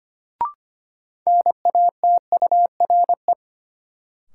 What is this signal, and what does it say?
Morse code, heard as a keyed tone, about 25 wpm, 700 Hz. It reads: NATURE
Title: Sentences from Top 600 Words - New Word Drill - 25wpm